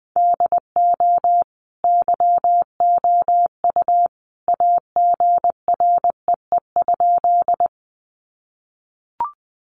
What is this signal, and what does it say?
Morse code, heard as a keyed tone, about 20 wpm, 700 Hz. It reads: DO YOU AGREE?
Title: Sentences from Top 1100 Words 20wpm